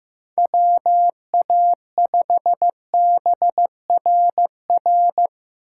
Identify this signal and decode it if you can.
Morse code, heard as a keyed tone, about 15 wpm, 700 Hz. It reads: WA5BRR
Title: USA Call Signs - 8x Character Spacing - 15wpm